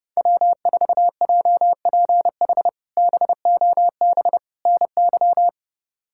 Morse code, heard as a keyed tone, about 32 wpm, 700 Hz. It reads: W4JPH BOB NY
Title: K1USN SST QSO Practice - 30wpm